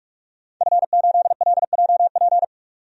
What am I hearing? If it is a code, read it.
Morse code, heard as a keyed tone, about 41 wpm, 700 Hz. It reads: F8LJP
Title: DXCC-100 Call Signs - 5 Characters - 45wpm